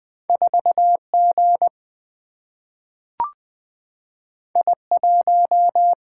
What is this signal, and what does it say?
Morse code, heard as a keyed tone, about 20 wpm, 700 Hz. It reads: 4G I1
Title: Single-Letter Single-Number 20wpm